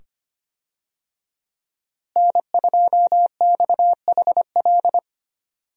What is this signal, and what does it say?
Morse code, heard as a keyed tone, about 25 wpm, 700 Hz. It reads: N2XHL
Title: USA Call Signs - 5 Characters - Repeated 2x - 25wpm